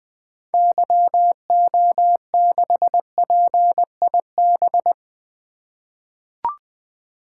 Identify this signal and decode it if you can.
Morse code, heard as a keyed tone, about 20 wpm, 700 Hz. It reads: YO6PIB
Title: 2023 SOTA Activator Call Signs - 20wpm